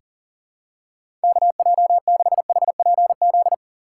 Morse code, heard as a keyed tone, about 40 wpm, 690 Hz. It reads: KJ6HPZ
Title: USA Call Signs - 4x Character Spacing - 40wpm